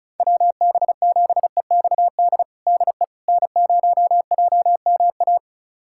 Morse code, heard as a keyed tone, about 36 wpm, 700 Hz. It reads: WB7EXD DE N0JMA
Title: USA Callsign DE Callsign 35wpm